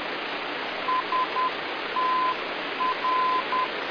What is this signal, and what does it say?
Morse code, heard as a keyed tone, about 10 wpm, 1020 Hz, with noise. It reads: STR